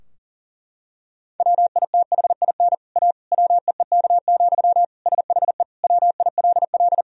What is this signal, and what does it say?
Morse code, heard as a keyed tone, about 40 wpm, 700 Hz. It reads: WITHIN A WEEK, SHE WILL
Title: Sentences from Top 1000 Words - Review Part 4 - 40wpm